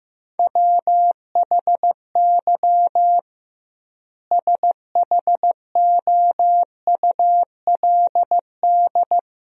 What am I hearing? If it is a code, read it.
Morse code, heard as a keyed tone, about 15 wpm, 700 Hz. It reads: WHY SHOULD
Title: 2x Word-Spacing - Sentences from Top 300 Words - 15wpm